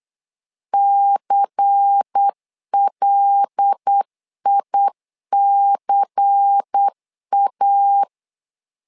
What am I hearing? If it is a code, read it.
Morse code, heard as a keyed tone, about 8 wpm, 800 Hz. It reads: CLICA
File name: LEZIONI / 5 LBVC / 5 LBVC 8wpm 2935